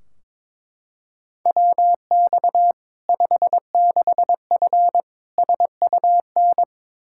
Morse code, heard as a keyed tone, about 22 wpm, 700 Hz. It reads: WX 56F SUN
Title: Practice QSO - Exchange 2 - Info - 22wpm